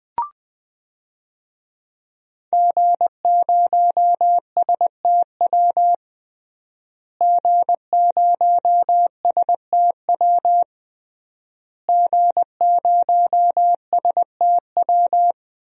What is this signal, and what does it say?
Morse code, heard as a keyed tone, about 21 wpm, 700 Hz. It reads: G0STW G0STW G0STW
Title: DXCC-100 Call Signs - 5 Characters - Repeated 3x - 20wpm